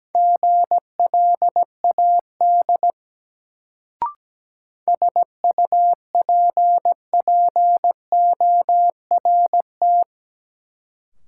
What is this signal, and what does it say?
Morse code, heard as a keyed tone, about 17 wpm, 700 Hz. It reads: GLAD SUPPORT
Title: Sentences from Top 800 Words - New Word Drill - 17wpm